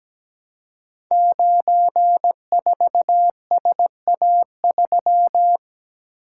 Morse code, heard as a keyed tone, about 17 wpm, 700 Hz. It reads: 94SA3
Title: Code Groups - 5 Letter:Numbers - 17wpm